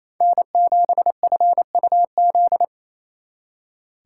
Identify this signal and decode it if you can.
Morse code, heard as a keyed tone, about 36 wpm, 700 Hz. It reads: N7FUZ